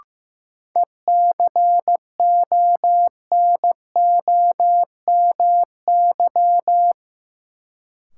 Morse code, heard as a keyed tone, about 15 wpm, 700 Hz. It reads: ECONOMY
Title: Sentences from Top 1200 Words - New Word Drill Alt - 15wpm